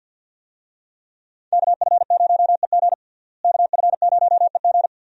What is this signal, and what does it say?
Morse code, heard as a keyed tone, about 52 wpm, 700 Hz. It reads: KF0EG KF0EG
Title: USA Call Signs - 5 Characters - Repeated 2x - 50wpm